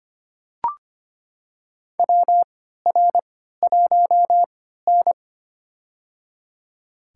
Morse code, heard as a keyed tone, about 25 wpm, spaced as for 14 wpm, 700 Hz. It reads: WR1N
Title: USA Call Signs - 4 Characters - 3x Character Spacing - 25wpm